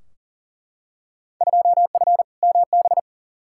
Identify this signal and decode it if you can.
Morse code, heard as a keyed tone, about 40 wpm, 700 Hz. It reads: 2F MB
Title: Field Day Exchanges Encore - 40wpm